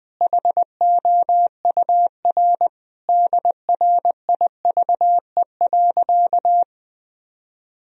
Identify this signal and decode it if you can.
Morse code, heard as a keyed tone, about 21 wpm, 700 Hz. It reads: HOUR DRIVE.
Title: Sentences from Top 1000 Words - Review Part 2 - 20wpm